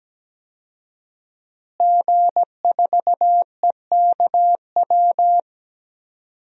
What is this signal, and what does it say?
Morse code, heard as a keyed tone, about 17 wpm, 700 Hz. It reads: G4EKW
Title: DXCC-100 Call Signs - 5 Characters - Repeated 3x - 17wpm